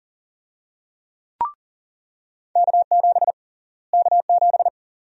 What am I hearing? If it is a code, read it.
Morse code, heard as a keyed tone, about 40 wpm, 700 Hz. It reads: K7 K7
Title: USA Call Sign Prefixes - Repeated 2x - 40wpm